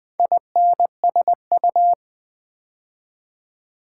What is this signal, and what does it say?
Morse code, heard as a keyed tone, about 20 wpm, 700 Hz. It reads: INSU
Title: Build A Word - 2 Letters At A Time - 6-Letter Words - 20wpm